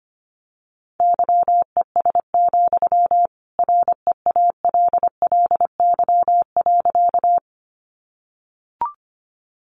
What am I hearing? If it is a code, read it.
Morse code, heard as a keyed tone, about 25 wpm, 700 Hz. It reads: YES, REALLY.
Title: Sentences from Top 600 Words 25wpm